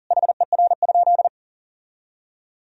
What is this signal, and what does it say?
Morse code, heard as a keyed tone, about 40 wpm, 700 Hz. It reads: HER?
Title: Sentences from Top 1000 Words - Review Part 3 - 40wpm